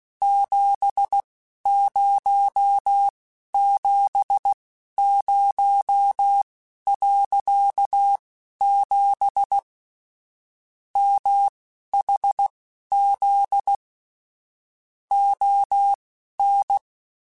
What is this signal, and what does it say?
Morse code, heard as a keyed tone, about 16 wpm, spaced as for 11 wpm, 800 Hz. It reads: 7070.7 MHZ ON
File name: QSOs / QSOs-mp3s / kc7haa@10